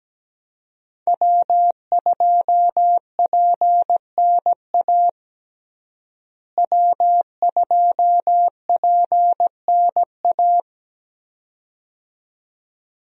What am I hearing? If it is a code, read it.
Morse code, heard as a keyed tone, about 17 wpm, 700 Hz. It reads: W2PNA W2PNA
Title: USA Call Signs - 5 Characters - Repeated 2x - 17wpm